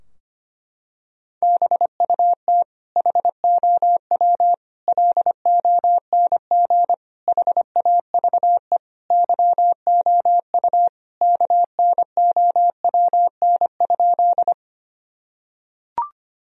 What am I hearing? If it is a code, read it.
Morse code, heard as a keyed tone, about 25 wpm, 700 Hz. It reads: BUT HOW LONG HAVE YOU KNOWN?
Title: Sentences from Top 1000 Words - Review Part 3 - 25wpm